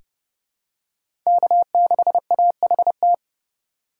Morse code, heard as a keyed tone, about 30 wpm, 700 Hz. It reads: K6AHT
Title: USA Call Signs - 5 Characters - 2x Character Spacing - 30wpm